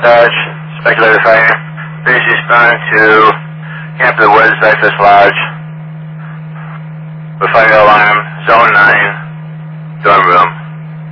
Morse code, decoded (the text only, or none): none